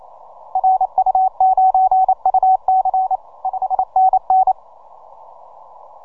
Morse code, heard as a keyed tone, about 29 wpm, 750 Hz. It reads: RU9UC 5NN
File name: simple-file-list / ZL7E 17MCW